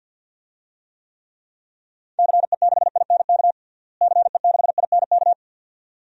Morse code, heard as a keyed tone, about 49 wpm, 700 Hz. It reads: CE6INK CE6INK
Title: DXCC-100 Call Signs - 6 Characters - Repeated 2x - 50wpm